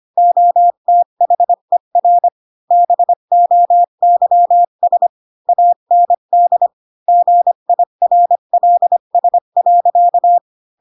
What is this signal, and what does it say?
Morse code, heard as a keyed tone, about 25 wpm, 700 Hz. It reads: OTHER BOYS AND GIRLS.
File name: Pinocchio / Pinocchio ch4 20 25 ik1hge